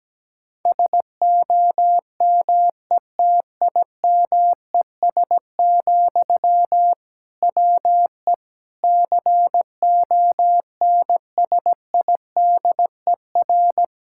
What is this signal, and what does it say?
Morse code, heard as a keyed tone, about 17 wpm, 700 Hz. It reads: SOMETIMES, WE CONSIDER
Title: Sentences from Top 1000 Words - Review Part 5 - 17wpm